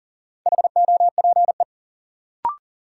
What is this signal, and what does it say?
Morse code, heard as a keyed tone, about 38 wpm, 700 Hz. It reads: HOPE
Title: Sentences from Top 1000 Words - New Word Drill - 40wpm